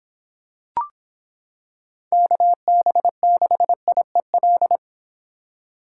Morse code, heard as a keyed tone, about 26 wpm, 700 Hz. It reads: KB6IEL